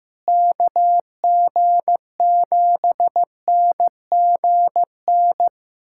Morse code, heard as a keyed tone, about 15 wpm, 700 Hz. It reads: KG7NGN
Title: USA Callsign DE Callsign - 6 Characters - 15wpm